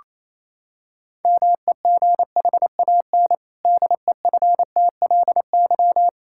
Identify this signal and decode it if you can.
Morse code, heard as a keyed tone, about 28 wpm, 700 Hz. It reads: MEGHAN DEFTLY
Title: Pangram Challenge v2 - 28wpm